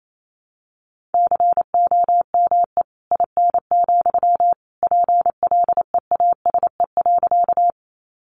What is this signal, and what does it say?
Morse code, heard as a keyed tone, about 28 wpm, 700 Hz. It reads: COME IN, PLEASE.
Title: Sentences from Top 1500 Words 28wpm